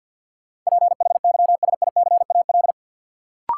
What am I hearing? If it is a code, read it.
Morse code, heard as a keyed tone, about 53 wpm, 700 Hz. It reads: PHYSICAL
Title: Sentences from Top 1400 Words - New Word Drill Alt - 50wpm